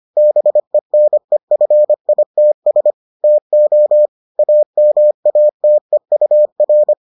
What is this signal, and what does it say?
Morse code, heard as a keyed tone, about 25 wpm, 600 Hz. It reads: BENEFITS TO AMATEUR